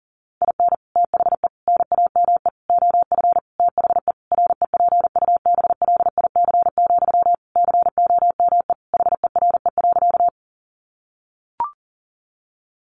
Morse code, heard as a keyed tone, about 40 wpm, 700 Hz. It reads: IN THE NAME OF THE REPUBLIC, COME HERE.
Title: Sentences from Top 1500 Words 40wpm